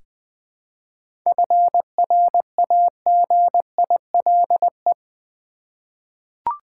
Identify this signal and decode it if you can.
Morse code, heard as a keyed tone, about 20 wpm, 700 Hz. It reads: FRAGILE
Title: 7 Letter Words - First 3 Letters Given - 20wpm